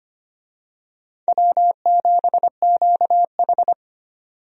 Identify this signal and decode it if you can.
Morse code, heard as a keyed tone, about 25 wpm, 700 Hz. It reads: W7QH
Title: USA Call Signs - 4 Characters - 25wpm